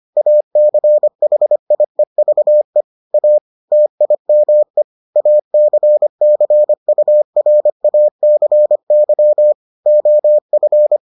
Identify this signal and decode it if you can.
Morse code, heard as a keyed tone, about 25 wpm, 600 Hz. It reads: ACHIEVE A TIME ACCURACY OF